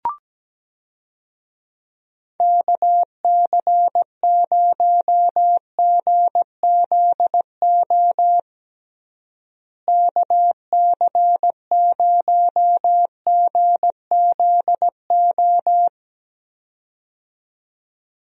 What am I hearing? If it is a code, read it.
Morse code, heard as a keyed tone, about 17 wpm, 700 Hz. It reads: KC0GZO KC0GZO
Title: USA Call Signs - 6 Characters - Repeated 2x - 17wpm